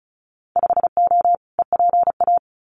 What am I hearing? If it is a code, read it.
Morse code, heard as a keyed tone, about 36 wpm, 700 Hz. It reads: 5O EPA
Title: Winter Field Day Exchanges Encore 35wpm